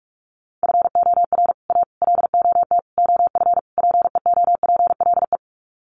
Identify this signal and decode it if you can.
Morse code, heard as a keyed tone, about 44 wpm, 700 Hz. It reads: FOR A LOT OF PEOPLE
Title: Sets of 5 Words Encore 45wpm